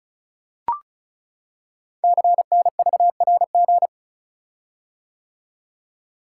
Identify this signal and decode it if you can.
Morse code, heard as a keyed tone, about 36 wpm, 690 Hz. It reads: CNVRG
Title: Code Groups - 5 Letters - 35wpm